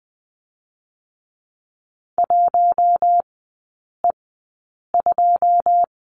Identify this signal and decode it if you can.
Morse code, heard as a keyed tone, about 20 wpm, spaced as for 8 wpm, 700 Hz. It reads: 1E2
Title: 121 Lesson 20e - Number 3 - Three Characters 20wpm